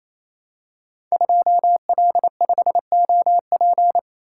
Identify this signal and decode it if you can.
Morse code, heard as a keyed tone, about 27 wpm, 700 Hz. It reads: 2L5OP
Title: Code Groups - 5 Letter:Numbers - 28wpm